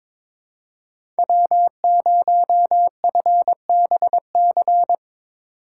Morse code, heard as a keyed tone, about 22 wpm, 700 Hz. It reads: W0FBC
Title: USA Call Signs - 5 Characters - 2x Character Spacing - 22wpm